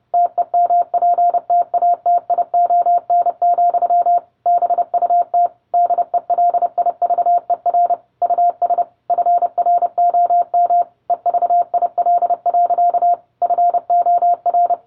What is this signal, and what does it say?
Morse code, heard as a keyed tone, about 30 wpm, 700 Hz. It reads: TEMPTATION, BUT DELIVER US FROM EVIL. FOR